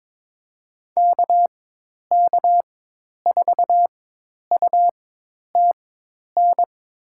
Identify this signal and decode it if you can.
Morse code, heard as a keyed tone, about 21 wpm, spaced as for 10 wpm, 700 Hz. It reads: KK4UTN
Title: USA Call Signs - 6 Characters - 4x Character Spacing - 22wpm